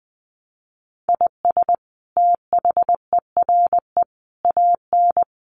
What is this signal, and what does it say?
Morse code, heard as a keyed tone, about 20 wpm, 700 Hz. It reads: IS THERE AN
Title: Sentences from Top 1300 Words 20wpm